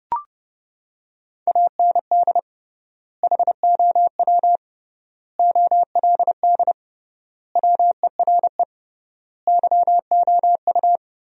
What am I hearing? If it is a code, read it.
Morse code, heard as a keyed tone, about 31 wpm, 700 Hz. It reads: AND HOW OLD WERE YOU
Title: 3x Word-Spacing - Sentences from Top 200 Words - 30wpm